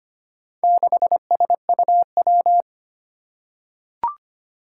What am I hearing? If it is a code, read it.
Morse code, heard as a keyed tone, about 25 wpm, 700 Hz. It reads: 6SUW
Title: Four Challenging Letter-Number Combinations 25wpm